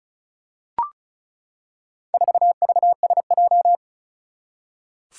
Morse code, heard as a keyed tone, about 36 wpm, 700 Hz. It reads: 4VSJ
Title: Four Challenging Letter-Number Combinations 35wpm